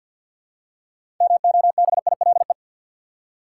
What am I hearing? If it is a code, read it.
Morse code, heard as a keyed tone, about 53 wpm, 700 Hz. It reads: MOBILE
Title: Six Letter Words - 50wpm